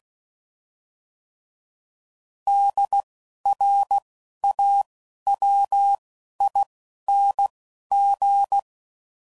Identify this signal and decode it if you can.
Morse code, heard as a keyed tone, about 16 wpm, 800 Hz. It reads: D R A W I N G